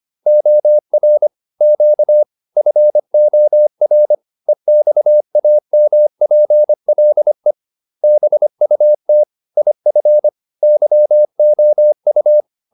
Morse code, heard as a keyed tone, about 25 wpm, 600 Hz. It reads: OR Q FOR EXAMPLE BUT IF YOU
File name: SS315 25